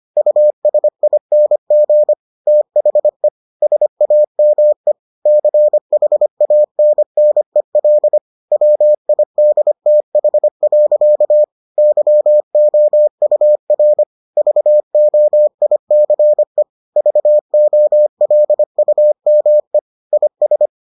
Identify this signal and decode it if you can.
Morse code, heard as a keyed tone, about 25 wpm, 600 Hz. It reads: USING THE SAME CHANNEL WIDTH. YOUR VOICE VOLUME IS